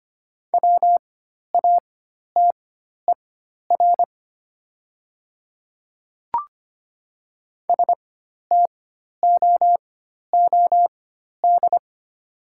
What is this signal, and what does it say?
Morse code, heard as a keyed tone, about 25 wpm, spaced as for 12 wpm, 700 Hz. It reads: WATER STOOD
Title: 4x Character Spacing - Top 500 Words - 25wpm